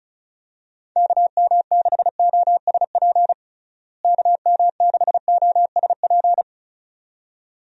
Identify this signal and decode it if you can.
Morse code, heard as a keyed tone, about 34 wpm, 700 Hz. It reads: KM6OSP KM6OSP
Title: USA Call Signs - 6 Characters - Repeated 2x - 35wpm